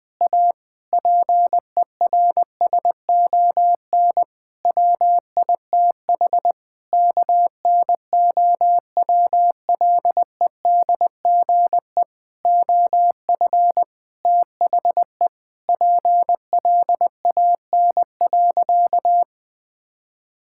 Morse code, heard as a keyed tone, about 20 wpm, 700 Hz. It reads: A PERSON WITH KNOWLEDGE OF THE PLAN.
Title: Sentences with 8-Word Phrases - 20wpm